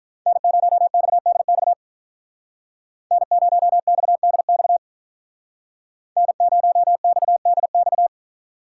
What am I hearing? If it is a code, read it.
Morse code, heard as a keyed tone, about 46 wpm, 700 Hz. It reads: N0XDX N0XDX N0XDX